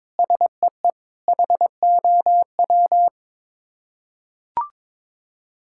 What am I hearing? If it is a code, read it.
Morse code, heard as a keyed tone, about 22 wpm, 700 Hz. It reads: SEE HOW
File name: Sets of 7 Words - 22wpm